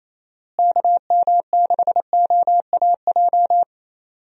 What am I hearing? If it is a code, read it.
Morse code, heard as a keyed tone, about 28 wpm, 700 Hz. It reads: KM6OAJ
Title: USA Call Signs - 6 Characters - Repeated 3x - 28wpm